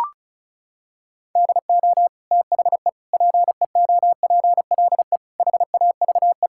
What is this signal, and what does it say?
Morse code, heard as a keyed tone, about 34 wpm, 700 Hz. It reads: DO THE PEOPLE HAVE
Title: Sentences from Top 1000 Words - Review Part 5 - 35wpm